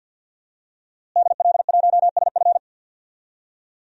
Morse code, heard as a keyed tone, about 52 wpm, 700 Hz. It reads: DL1SF
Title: DXCC-100 Call Signs - 5 Characters - Repeated 2x - 50wpm